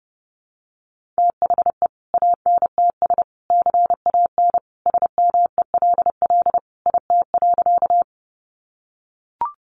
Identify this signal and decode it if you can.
Morse code, heard as a keyed tone, about 30 wpm, 700 Hz. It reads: THE ANTS CAN SMELL IT.
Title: Additional Easy Phrases 30wpm